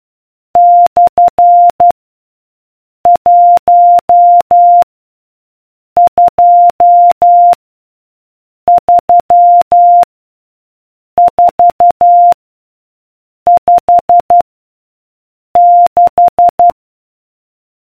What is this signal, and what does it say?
Morse code, heard as a keyed tone, about 12 wpm, 700 Hz. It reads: / 1 2 3 4 5 6